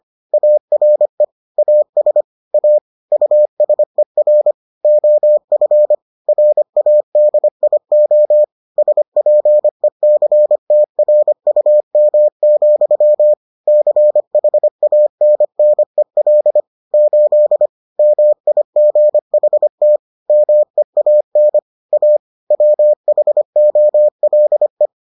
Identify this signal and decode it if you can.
Morse code, heard as a keyed tone, about 25 wpm, 600 Hz. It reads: ARE AS A USER OF RADIO SPECTRUM, CHANNEL 8 MIGHT MEAN A WHOLE